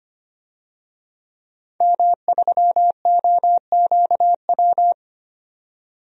Morse code, heard as a keyed tone, about 25 wpm, 700 Hz. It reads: M3OQW
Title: DXCC-100 Call Signs - 5 Characters - 25wpm